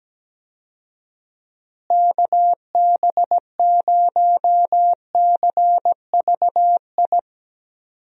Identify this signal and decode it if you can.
Morse code, heard as a keyed tone, about 17 wpm, 700 Hz. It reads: KB0CVI
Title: USA Call Signs - 6 Characters - Repeated 2x - 17wpm